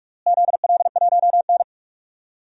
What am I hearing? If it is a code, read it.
Morse code, heard as a keyed tone, about 41 wpm, 700 Hz. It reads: ZL1N
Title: DXCC-100 Call Signs - 4 Characters - Repeated 3x - 45wpm